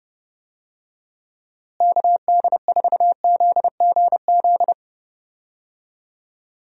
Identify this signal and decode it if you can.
Morse code, heard as a keyed tone, about 30 wpm, 700 Hz. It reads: KD4ZGZ
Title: USA Call Signs - 6 Characters - Repeated 3x - 30wpm